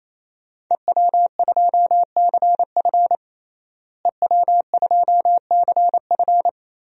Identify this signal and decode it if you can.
Morse code, heard as a keyed tone, about 28 wpm, 700 Hz. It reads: EW2CF EW2CF
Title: DXCC-100 Call Signs - 5 Characters - Repeated 3x - 28wpm